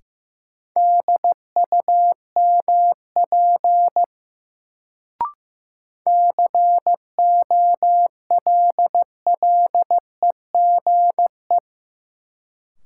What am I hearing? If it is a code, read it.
Morse code, heard as a keyed tone, about 15 wpm, 700 Hz. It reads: DUMP COLLEGE
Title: Sentences from Top 1300 Words - New Word Drill - 15wpm